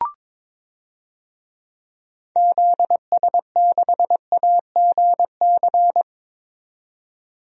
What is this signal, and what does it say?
Morse code, heard as a keyed tone, about 22 wpm, 700 Hz. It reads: ZS6AGC
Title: DXCC-100 Call Signs - 6 Characters - 22wpm